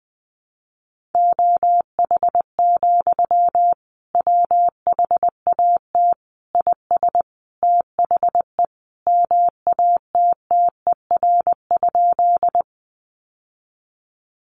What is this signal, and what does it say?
Morse code, heard as a keyed tone, about 20 wpm, 700 Hz. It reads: OH, WHAT IS THE MATTER?
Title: Sentences from Top 700 Words 20wpm